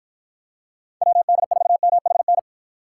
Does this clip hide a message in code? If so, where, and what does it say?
Yes: Morse code from 1.0 to 2.4 s, heard as a keyed tone, about 51 wpm, 700 Hz. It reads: WD4MHN